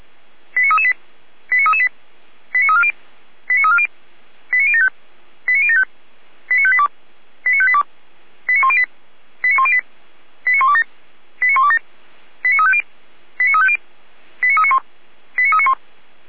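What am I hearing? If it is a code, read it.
Five-tone selective call (ZVEI), heard as quick runs of five tones, tones of about 70 ms. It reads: 89338 89338 89340 89340 89075 89075 89582 89582 89108 89108 89127 89127 89350 89350 89391 89391
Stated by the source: ZVEI five-tone selective call received on SDR.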